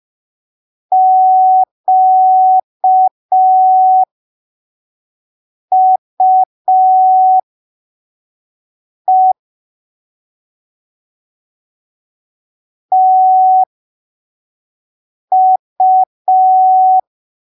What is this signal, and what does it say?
Morse code, heard as a keyed tone, about 5 wpm, spaced as for 3 wpm, 750 Hz. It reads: QUE TU
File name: arquivos / texto / texto 26 05ppm